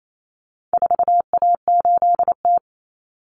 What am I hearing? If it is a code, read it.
Morse code, heard as a keyed tone, about 28 wpm, 700 Hz. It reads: 4A8T
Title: Common Cut Numbers - 4-Digit 28wpm